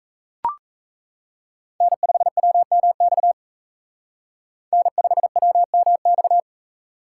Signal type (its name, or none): Morse code